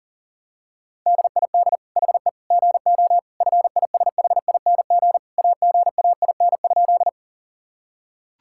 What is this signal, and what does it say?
Morse code, heard as a keyed tone, about 40 wpm, 700 Hz. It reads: DID HE GO FISHING AGAIN?
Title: Sentences from Top 1000 Words - Review Part 5 - 40wpm